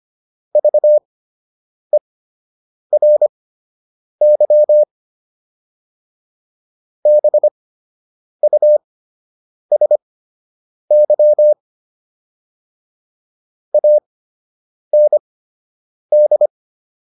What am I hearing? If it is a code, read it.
Morse code, heard as a keyed tone, about 25 wpm, spaced as for 8 wpm, 600 Hz. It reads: VERY BUSY AND